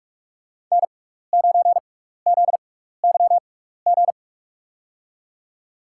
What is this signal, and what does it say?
Morse code, heard as a keyed tone, about 45 wpm, spaced as for 15 wpm, 700 Hz. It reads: N9ZYG
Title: USA Call Signs - 6x Character Spacing - 45wpm